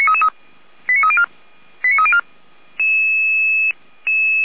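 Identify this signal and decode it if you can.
Five-tone selective call (ZVEI), heard as quick runs of five tones, tones of about 70 ms. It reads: 89304 89304
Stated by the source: ZVEI five-tone selective call received on SDR, from the Feuerwehr Orlamünde fire brigade server.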